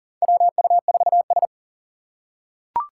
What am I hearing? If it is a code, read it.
Morse code, heard as a keyed tone, about 40 wpm, 700 Hz. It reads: WU4S
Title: USA Call Signs - 4 Characters - 4x Character Spacing - 40wpm